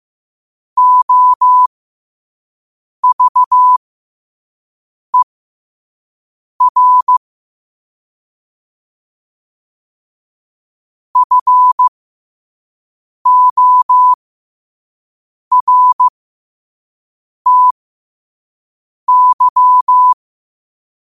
Morse code, heard as a keyed tone, about 15 wpm, spaced as for 5 wpm, 1000 Hz. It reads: OVER FORTY
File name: qotdfiles / 5wpm